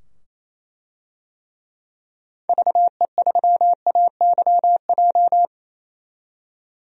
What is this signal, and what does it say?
Morse code, heard as a keyed tone, about 28 wpm, 700 Hz. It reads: VE3AYJ
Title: DXCC-100 Call Signs - 6 Characters - 28wpm